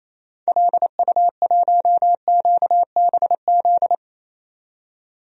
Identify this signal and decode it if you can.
Morse code, heard as a keyed tone, about 28 wpm, 700 Hz. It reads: LU1QBZ
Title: DXCC-100 Call Signs - 6 Characters - 28wpm